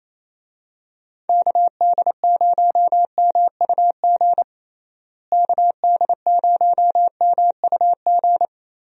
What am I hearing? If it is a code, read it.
Morse code, heard as a keyed tone, about 28 wpm, 700 Hz. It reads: KD0MUG KD0MUG
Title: USA Call Signs - 6 Characters - Repeated 3x - 28wpm